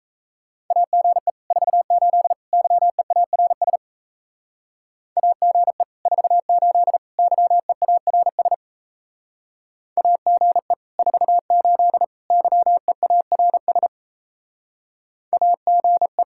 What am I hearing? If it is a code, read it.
Morse code, heard as a keyed tone, about 34 wpm, 700 Hz. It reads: AGE 48 YEARS AGE 48 YEARS AGE 48 YEARS AGE
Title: Speed-Racing Common QSO Phrases Encore - 28wpm